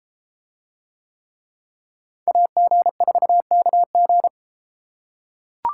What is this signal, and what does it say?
Morse code, heard as a keyed tone, about 33 wpm, 700 Hz. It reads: AG4KG